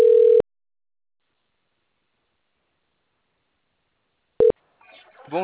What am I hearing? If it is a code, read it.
Ringback tone (North America), 440 + 480 Hz.